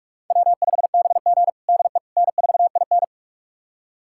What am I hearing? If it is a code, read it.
Morse code, heard as a keyed tone, about 42 wpm, 700 Hz. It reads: W5BG DE N4IN
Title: USA Callsign DE Callsign - 4 Characters - 45wpm